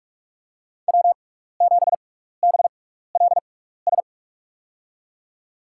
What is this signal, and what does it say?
Morse code, heard as a keyed tone, about 46 wpm, 700 Hz. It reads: W 7 B L S
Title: USA Call Signs - 5 Characters - 6x Character Spacing - 45wpm